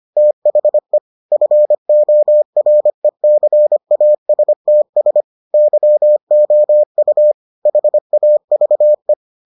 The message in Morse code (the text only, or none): THE FORECASTS YOU HAVE